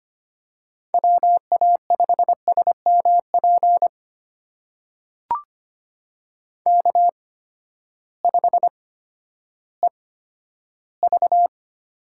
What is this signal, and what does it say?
Morse code, heard as a keyed tone, about 25 wpm, spaced as for 13 wpm, 700 Hz. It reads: WA5SMP K5EV
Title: USA Call Signs - 8x Character Spacing - 25wpm